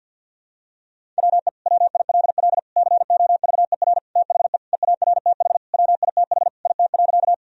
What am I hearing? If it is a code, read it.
Morse code, heard as a keyed tone, about 47 wpm, 700 Hz. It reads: WE WILL COVER THE EARTH WITH IT.
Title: Sentences from Top 1000 Words - Review Part 1 - 50wpm